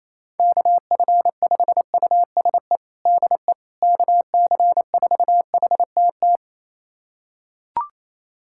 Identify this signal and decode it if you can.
Morse code, heard as a keyed tone, about 27 wpm, 700 Hz. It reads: KF5USE DE KC4HTT
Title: USA Callsign DE Callsign 28wpm